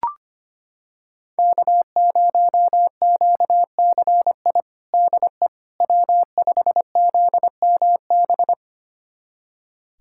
Morse code, heard as a keyed tone, about 25 wpm, 700 Hz. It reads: K0QCI DE W5ZMB
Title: USA Callsign DE Callsign - 5 Characters - 25wpm